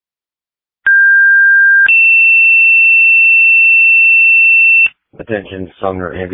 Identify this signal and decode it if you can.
Two-tone page, heard as a one-second tone, then a three-second tone, about 1600 then 2700 Hz, with speech. It is heard from 0.9 to 4.9 s.